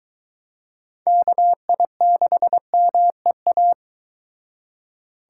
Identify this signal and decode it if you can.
Morse code, heard as a keyed tone, about 23 wpm, 700 Hz. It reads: KI6MEA